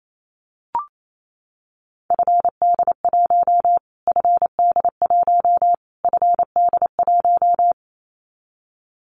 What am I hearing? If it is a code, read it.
Morse code, heard as a keyed tone, about 28 wpm, 700 Hz. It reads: FD1 FD1 FD1
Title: Callsign Prefixes for Top 25 DXCC - Repeated 3x - 28wpm